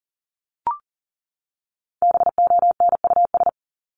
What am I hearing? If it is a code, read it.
Morse code, heard as a keyed tone, about 40 wpm, 700 Hz. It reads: BONUS
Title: Top 1751-2000 Words 40wpm